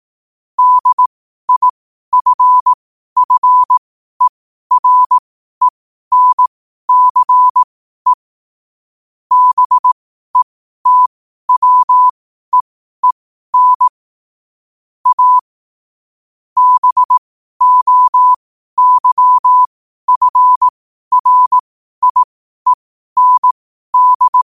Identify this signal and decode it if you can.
Morse code, heard as a keyed tone, about 18 wpm, spaced as for 12 wpm, 1000 Hz. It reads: DIFFERENCE BETWEEN A BOYFRIEND